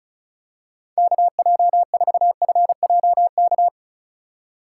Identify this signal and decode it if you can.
Morse code, heard as a keyed tone, about 36 wpm, 700 Hz. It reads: KJ4FJK
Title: USA Call Signs - 6 Characters - 4x Character Spacing - 35wpm